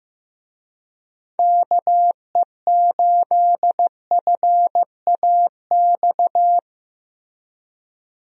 Morse code, heard as a keyed tone, about 15 wpm, 700 Hz. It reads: KE8FAX
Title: USA Call Signs - 6 Characters - Repeated 3x - 15wpm